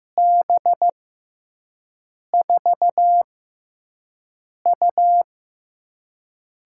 Morse code, heard as a keyed tone, about 15 wpm, spaced as for 5 wpm, 700 Hz. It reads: B4U
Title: USA Call Signs - 6x Character Spacing - 15wpm